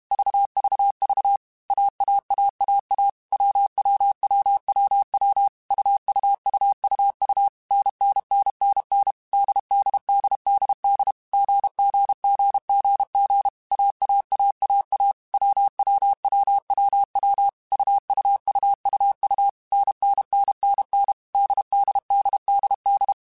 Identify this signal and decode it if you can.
Morse code, heard as a keyed tone, about 32 wpm, 800 Hz. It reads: VVV AAAAA WWWWW UUUUU NNNNN DDDDD GGGGG AAAAA WWWWW UUUUU NNNNN DDDDD
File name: Basic Sets / Basic Letter Groups Collective7 / AWUNDG@30